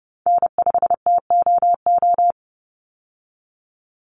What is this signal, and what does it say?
Morse code, heard as a keyed tone, about 30 wpm, 700 Hz. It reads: N5TOO
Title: Call Signs Repeated 3x - 30wpm